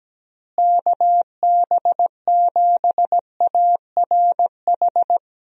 Morse code, heard as a keyed tone, about 17 wpm, 700 Hz. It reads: KB7ARH
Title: USA Call Signs - 6 Characters - Repeated 3x - 17wpm